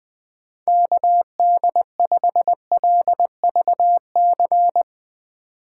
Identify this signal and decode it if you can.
Morse code, heard as a keyed tone, about 20 wpm, 700 Hz. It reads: KD5LVC
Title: Speed Racing - USA 2x3 Call Signs 17wpm